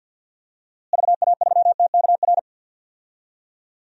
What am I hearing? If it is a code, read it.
Morse code, heard as a keyed tone, about 46 wpm, 700 Hz. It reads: VA3TKR
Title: DXCC-100 Call Signs - 6 Characters - Repeated 2x - 50wpm